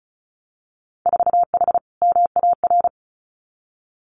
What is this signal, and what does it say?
Morse code, heard as a keyed tone, about 36 wpm, 700 Hz. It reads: 4H MAR
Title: Winter Field Day Exchanges Encore 35wpm